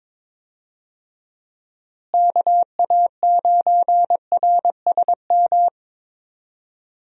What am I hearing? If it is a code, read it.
Morse code, heard as a keyed tone, about 22 wpm, 700 Hz. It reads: KA9RSM
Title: USA Call Signs - 6 Characters - Repeated 3x - 22wpm